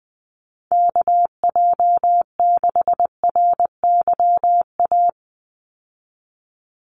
Morse code, heard as a keyed tone, about 20 wpm, 700 Hz. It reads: KJ6RYA
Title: USA Call Signs 20wpm